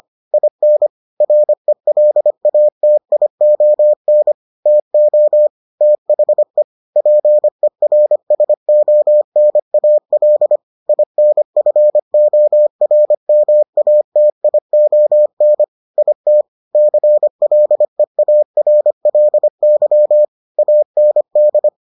Morse code, heard as a keyed tone, about 25 wpm, 600 Hz. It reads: IN RELATION TO THE PERSONAL INFORMATION IT CLEARLY AND